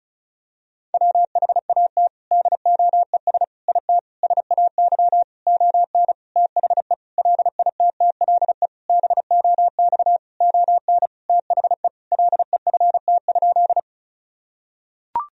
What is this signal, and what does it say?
Morse code, heard as a keyed tone, about 36 wpm, 700 Hz. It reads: WHAT DOES IT SAY ON THE LITTLE BOX ON THE LEFT?
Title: Sentences from Top 1000 Words - Review Part 2 - 35wpm